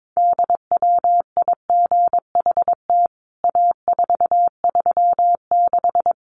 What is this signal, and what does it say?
Morse code, heard as a keyed tone, about 22 wpm, 700 Hz. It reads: DWIGHT A436
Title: CWT Exchange 22wpm